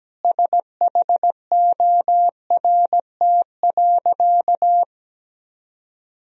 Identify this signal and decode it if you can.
Morse code, heard as a keyed tone, about 17 wpm, 700 Hz. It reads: SHORT.
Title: Sentences from Top 500 Words and Abbreviations 17wpm